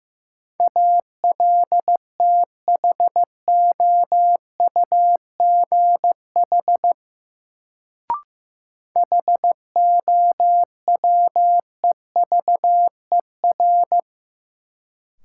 Morse code, heard as a keyed tone, about 15 wpm, 700 Hz. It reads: ALTHOUGH HOWEVER
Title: Sentences from Top 600 Words - New Word Drill - 15wpm